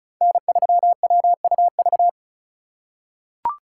Morse code, heard as a keyed tone, about 34 wpm, 700 Hz. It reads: N3WUV
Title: USA Call Signs - 5 Characters - 2x Character Spacing - 35wpm